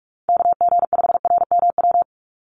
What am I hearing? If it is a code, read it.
Morse code, heard as a keyed tone, about 44 wpm, 700 Hz. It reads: KG5RMW
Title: USA Call Signs 45wpm